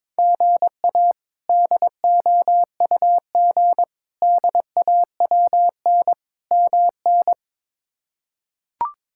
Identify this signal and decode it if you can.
Morse code, heard as a keyed tone, about 22 wpm, 700 Hz. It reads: GA DOUG DAWN MN
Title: K1USN SST QSO Practice - 22wpm